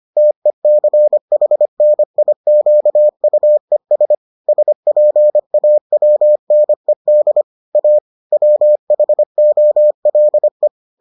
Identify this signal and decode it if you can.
Morse code, heard as a keyed tone, about 25 wpm, 600 Hz. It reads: TECHNIQUES SPAWNED A WHOLE